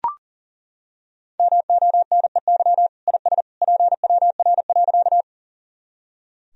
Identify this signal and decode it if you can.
Morse code, heard as a keyed tone, about 40 wpm, 700 Hz. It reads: MONEY IS PWR.
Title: Sentences from Top 500 Words and Abbreviations 40wpm